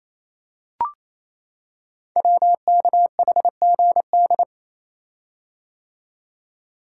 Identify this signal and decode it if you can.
Morse code, heard as a keyed tone, about 29 wpm, 700 Hz. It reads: WKHGD
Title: Code Groups - 5 Letters - 28wpm